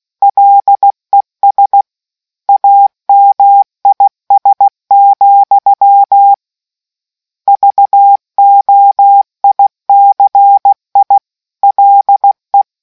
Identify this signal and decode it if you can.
Morse code, heard as a keyed tone, about 16 wpm, 800 Hz. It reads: LES AMIS, VOICI LE